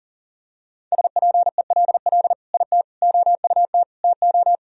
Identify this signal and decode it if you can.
Morse code, heard as a keyed tone, about 37 wpm, 700 Hz. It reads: SPELL IT OUT TO
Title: Sentences from Top 1000 Words - Review Part 1 - 40wpm